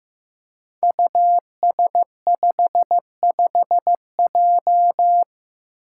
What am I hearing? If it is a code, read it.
Morse code, heard as a keyed tone, about 15 wpm, 700 Hz. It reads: US55J
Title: Five Challenging Letter-Number Combinations 15wpm